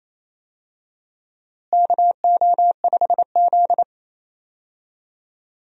Morse code, heard as a keyed tone, about 27 wpm, 700 Hz. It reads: KO5Z
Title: USA Call Signs - 4 Characters - Repeated 3x - 28wpm